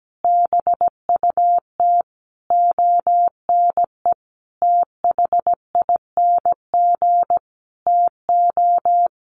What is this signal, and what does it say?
Morse code, heard as a keyed tone, about 17 wpm, 700 Hz. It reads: BUT ONE THING TO
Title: Sentences from Top 100 Words 17wpm